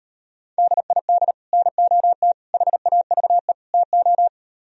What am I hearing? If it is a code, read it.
Morse code, heard as a keyed tone, about 38 wpm, 700 Hz. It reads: DID NOT HAVE TO